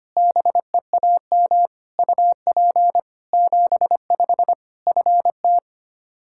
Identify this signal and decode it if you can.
Morse code, heard as a keyed tone, about 25 wpm, 700 Hz. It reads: BEAM UP 75 FT
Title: Practice QSO - Exchange 2 - Info - 25wpm